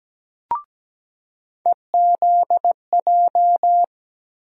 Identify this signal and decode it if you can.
Morse code, heard as a keyed tone, about 17 wpm, 700 Hz. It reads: EZJ
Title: USA Call Sign Postfixes - 3 Letters 17wpm